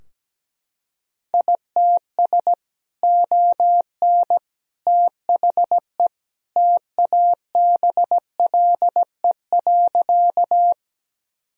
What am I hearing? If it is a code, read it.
Morse code, heard as a keyed tone, about 17 wpm, 700 Hz. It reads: ITS ON THE TABLE.
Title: Sentences from Top 1000 Words - Review Part 2 - 17wpm